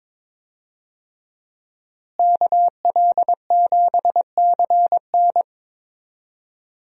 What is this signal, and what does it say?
Morse code, heard as a keyed tone, about 21 wpm, 700 Hz. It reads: KL7CN
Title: USA Call Signs - 5 Characters - Repeated 2x - 22wpm